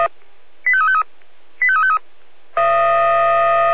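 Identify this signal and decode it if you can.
Five-tone selective call (ZVEI), heard as quick runs of five tones, tones of about 70 ms. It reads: 85353 85353
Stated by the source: ZVEI five-tone call received by SDR.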